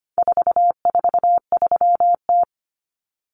Morse code, heard as a keyed tone, about 25 wpm, 700 Hz. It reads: 443T
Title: Common Cut Numbers - 4-Digit 25wpm